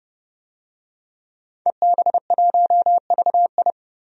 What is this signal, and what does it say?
Morse code, heard as a keyed tone, about 30 wpm, 700 Hz. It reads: EB1VI
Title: DXCC-100 Call Signs - 5 Characters - 30wpm